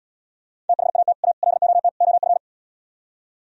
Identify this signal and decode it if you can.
Morse code, heard as a keyed tone, about 77 wpm, 700 Hz. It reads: THAT A LOT OF